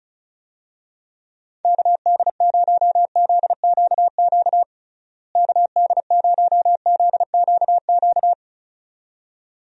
Morse code, heard as a keyed tone, about 36 wpm, 700 Hz. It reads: KD0ZQQ KD0ZQQ
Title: USA Call Signs - 6 Characters - Repeated 2x - 35wpm